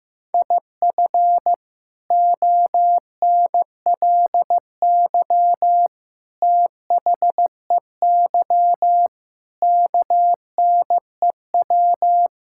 Morse code, heard as a keyed tone, about 15 wpm, 700 Hz. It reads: IF ONLY THEY KNEW
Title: Sentences from Top 1000 Words - Review Part 6 - 15wpm